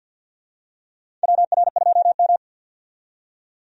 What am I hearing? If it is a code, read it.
Morse code, heard as a keyed tone, about 52 wpm, 700 Hz. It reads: WR2M
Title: USA Call Signs - 4 Characters - 50wpm